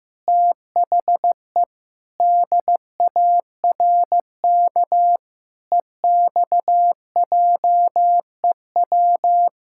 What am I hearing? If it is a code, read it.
Morse code, heard as a keyed tone, about 15 wpm, 700 Hz. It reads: THE DARK EXJEW
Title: Pangram Challenge v2 - 15wpm